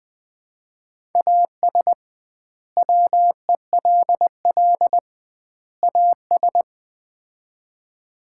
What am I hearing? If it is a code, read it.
Morse code, heard as a keyed tone, about 20 wpm, 700 Hz. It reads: AS WELL AS
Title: 2x Word-Spacing - Sets of 3 Words - 20wpm